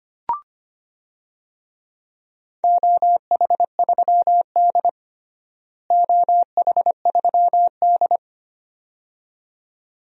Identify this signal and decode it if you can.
Morse code, heard as a keyed tone, about 25 wpm, 700 Hz. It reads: OH3D OH3D
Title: DXCC-100 Call Signs - 4 Characters - Repeated 2x - 25wpm